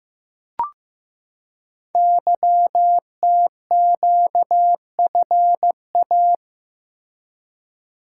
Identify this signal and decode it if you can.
Morse code, heard as a keyed tone, about 15 wpm, 700 Hz. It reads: YTQFA
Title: Code Groups - 5 Letters - 15wpm